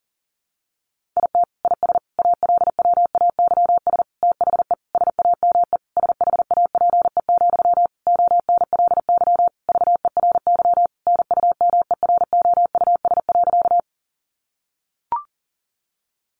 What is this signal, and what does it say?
Morse code, heard as a keyed tone, about 40 wpm, 700 Hz. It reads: IT IS ALWAYS THE SAME SHAPE, ONLY VERY NUMEROUS.
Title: Sentences from Top 1400 Words 40wpm